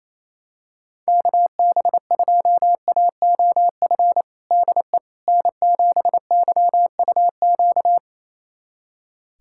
Morse code, heard as a keyed tone, about 28 wpm, 700 Hz. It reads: KB2AOF DE N7YUQ
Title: USA Callsign DE Callsign 28wpm